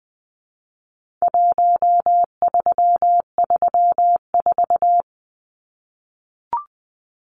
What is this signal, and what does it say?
Morse code, heard as a keyed tone, about 19 wpm, 700 Hz. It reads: 1334
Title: Numbers - Contest Style - 4-Digit 20wpm